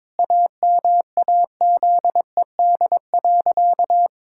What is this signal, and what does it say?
Morse code, heard as a keyed tone, about 22 wpm, 700 Hz. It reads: AMAZED.
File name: practice / 22wpm / Sentences with 4-Word Phrases - 22wpm